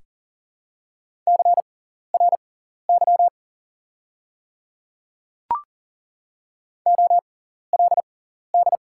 Morse code, heard as a keyed tone, about 40 wpm, spaced as for 14 wpm, 700 Hz. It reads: CRY OLD
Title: 6x Character Spacing - Top 500 Words - 40wpm